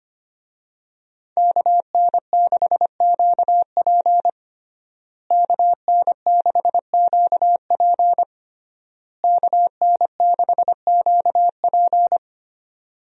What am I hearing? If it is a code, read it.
Morse code, heard as a keyed tone, about 25 wpm, 700 Hz. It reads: KN6QP KN6QP KN6QP
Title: USA Call Signs - 5 Characters - Repeated 3x - 25wpm